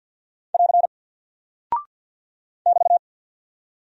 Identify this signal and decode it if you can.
Morse code, heard as a keyed tone, about 48 wpm, 700 Hz. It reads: <AR> <BT>
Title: Single Character (Letter:Number:Punctuation:Prosign) Alt 50wpm